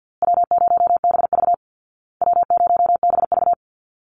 Morse code, heard as a keyed tone, about 48 wpm, 700 Hz. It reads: W0BV W0BV
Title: Easy Call Signs Repeated 3x - 50wpm